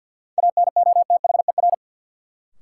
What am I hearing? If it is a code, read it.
Morse code, heard as a keyed tone, about 47 wpm, 700 Hz. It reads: ANOTHER
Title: Sets of 5 Words Encore - New Word Drill - 50wpm